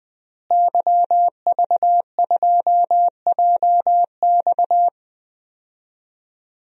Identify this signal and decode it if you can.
Morse code, heard as a keyed tone, about 20 wpm, 700 Hz. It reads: YV2JX
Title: DXCC-100 Call Signs - 5 Characters - Repeated 3x - 20wpm